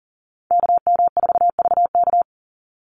Morse code, heard as a keyed tone, about 40 wpm, 700 Hz. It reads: KM4VK
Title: USA Call Signs 40wpm